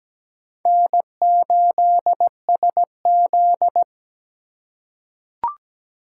Morse code, heard as a keyed tone, about 17 wpm, 700 Hz. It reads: N8SZ
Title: USA Call Signs - 4 Characters - 3x Character Spacing - 17wpm